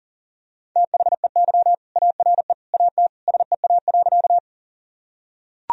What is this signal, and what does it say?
Morse code, heard as a keyed tone, about 40 wpm, 700 Hz. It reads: THEY ARE AT SEA.
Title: Sentences from Top 1000 Words - Review Part 1 - 40wpm